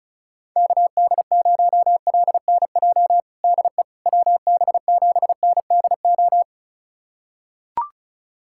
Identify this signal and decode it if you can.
Morse code, heard as a keyed tone, about 36 wpm, 700 Hz. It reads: KD0LNJ DE WB7NDO
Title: USA Callsign DE Callsign 35wpm